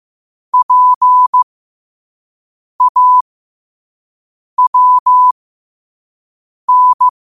Morse code, heard as a keyed tone, about 15 wpm, spaced as for 5 wpm, 1000 Hz. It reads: PAWN
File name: qotdfiles / 5wpm